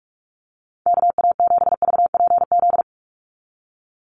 Morse code, heard as a keyed tone, about 44 wpm, 700 Hz. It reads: KA7VPZ
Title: USA Call Signs 45wpm